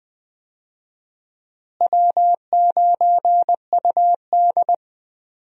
Morse code, heard as a keyed tone, about 20 wpm, 700 Hz. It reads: W9UD
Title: USA Call Signs - 4 Characters - Repeated 3x - 20wpm